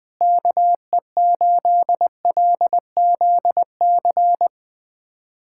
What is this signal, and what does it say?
Morse code, heard as a keyed tone, about 20 wpm, 700 Hz. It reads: KE8LZC
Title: USA Callsign DE Callsign - 6 Characters - 20wpm